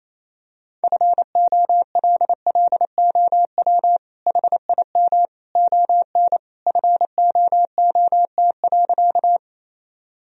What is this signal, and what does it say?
Morse code, heard as a keyed tone, about 28 wpm, 700 Hz. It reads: FOLLOW HIM ON FOOT.
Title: Sentences from Top 1000 Words - Review Part 3 - 28wpm